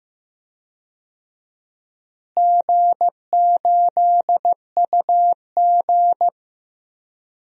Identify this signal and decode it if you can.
Morse code, heard as a keyed tone, about 15 wpm, 700 Hz. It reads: G8UG
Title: DXCC-100 Call Signs - 4 Characters - Repeated 2x - 15wpm